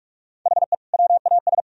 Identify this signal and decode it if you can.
Morse code, heard as a keyed tone, about 46 wpm, 700 Hz. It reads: HE WAS
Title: Sentences from Top 1000 Words - Review Part 3 - 45wpm